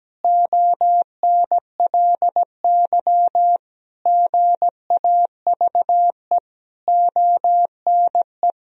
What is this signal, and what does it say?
Morse code, heard as a keyed tone, about 17 wpm, 700 Hz. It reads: ONLY GAVE ONE
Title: Sentences from Top 1000 Words - Review Part 2 - 17wpm